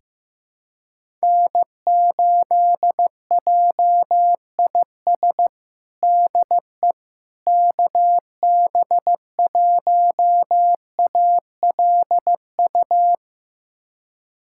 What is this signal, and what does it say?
Morse code, heard as a keyed tone, about 15 wpm, 700 Hz. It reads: N8JIS DE KB1ALU
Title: USA Callsign DE Callsign 15wpm